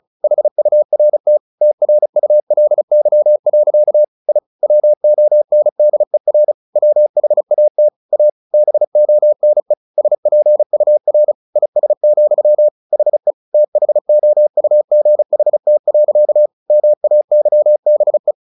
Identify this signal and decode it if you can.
Morse code, heard as a keyed tone, about 34 wpm, 600 Hz. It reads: HURT TRULY. I WONDER WHAT A BONE SPUR IS, HE THOUGHT. MAYBE
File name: the old man and the sea 2 35wpm